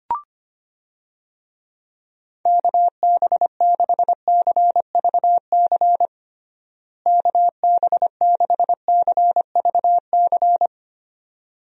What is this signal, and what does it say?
Morse code, heard as a keyed tone, about 25 wpm, 700 Hz. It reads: KB6CVC KB6CVC
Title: USA Call Signs - 6 Characters - Repeated 2x - 25wpm